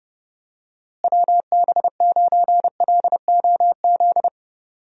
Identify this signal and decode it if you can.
Morse code, heard as a keyed tone, about 30 wpm, 700 Hz. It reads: WB9LOZ
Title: USA Call Signs - 6 Characters - Repeated 2x - 30wpm